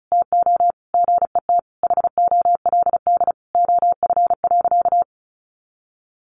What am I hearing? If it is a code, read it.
Morse code, heard as a keyed tone, about 34 wpm, 700 Hz. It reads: TO GET HOLD OF.
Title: A Sound Of Thunder 35wpm